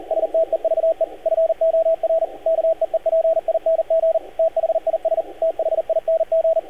Morse code, heard as a keyed tone, about 38 wpm, 650 Hz, with radio noise. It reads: STEVE FOR KEEPING THIS THING
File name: archived / CRSnet